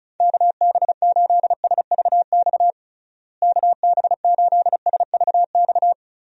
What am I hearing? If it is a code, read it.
Morse code, heard as a keyed tone, about 36 wpm, 700 Hz. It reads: KB8SVX KB8SVX
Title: USA Call Signs - 6 Characters - Repeated 2x - 35wpm